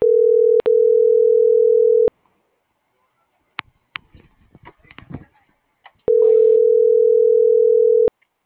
Ringback tone (North America), 440 + 480 Hz.